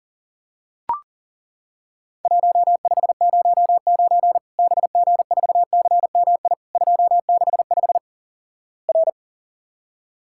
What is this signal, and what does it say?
Morse code, heard as a keyed tone, about 42 wpm, 700 Hz. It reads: 1509 BG4CMI 265 R
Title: Worked All Europe DX Contest - QTC Exchange - 40wpm